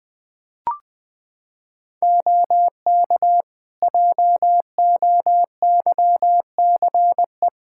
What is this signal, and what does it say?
Morse code, heard as a keyed tone, about 20 wpm, 700 Hz. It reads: OK JOYCE
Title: Practice QSO - Exchange 3 - More Info - 20wpm